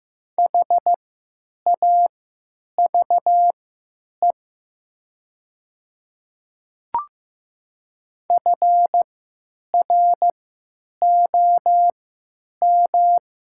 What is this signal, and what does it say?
Morse code, heard as a keyed tone, about 15 wpm, spaced as for 9 wpm, 690 Hz. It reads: HAVE FROM
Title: 3x Character Spacing - Top 100 Words - 15wpm